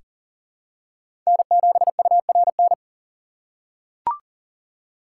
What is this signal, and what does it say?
Morse code, heard as a keyed tone, about 39 wpm, 700 Hz. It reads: N7URN
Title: USA Call Signs - 5 Characters - 6x Character Spacing - 40wpm